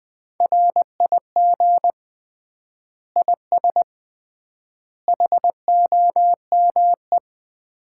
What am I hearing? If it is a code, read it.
Morse code, heard as a keyed tone, about 20 wpm, 700 Hz. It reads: RIG IS HOME
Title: Common QSO Phrases - 3x Word-Spacing - 20wpm